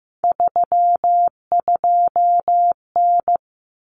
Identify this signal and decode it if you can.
Morse code, heard as a keyed tone, about 15 wpm, 700 Hz. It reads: 32N
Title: Common Cut Numbers - 3-Digit 15wpm